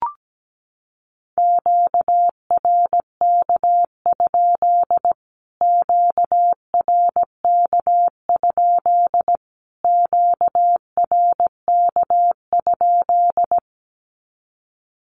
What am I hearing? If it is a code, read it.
Morse code, heard as a keyed tone, about 17 wpm, 700 Hz. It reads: QRK? QRK? QRK?
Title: Q-Codes Repeated 3x 17wpm